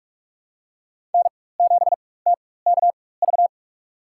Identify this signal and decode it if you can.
Morse code, heard as a keyed tone, about 46 wpm, spaced as for 21 wpm, 700 Hz. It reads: N7TKV
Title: USA Call Signs - 5 Characters - 4x Character Spacing - 45wpm